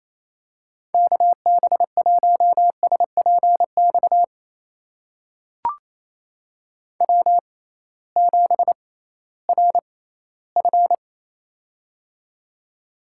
Morse code, heard as a keyed tone, about 28 wpm, spaced as for 16 wpm, 700 Hz. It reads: KB1SPX W7RF
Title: USA Call Signs - 6x Character Spacing - 28wpm